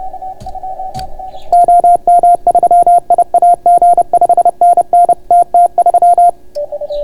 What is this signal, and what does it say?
Morse code, heard as a keyed tone, about 30 wpm, 690 Hz. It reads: OM3IAG 5NN TT3